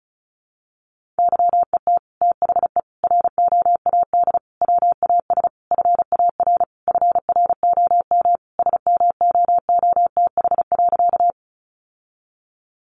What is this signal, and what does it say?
Morse code, heard as a keyed tone, about 34 wpm, 700 Hz. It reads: YET THE ROAD WAS FAR FROM SMOOTH.
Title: Sentences from Top 1400 Words 35wpm